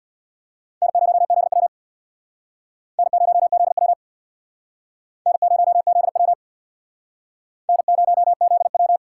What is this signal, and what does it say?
Morse code, heard as a keyed tone, about 64 wpm, 700 Hz. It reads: N0ZW N0ZW N0ZW N0ZW